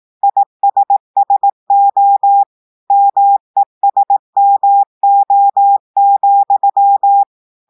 Morse code, heard as a keyed tone, about 18 wpm, 800 Hz. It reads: ISSO MESMO,
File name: portugais texte en clair 2 vitesse 18 mots minutes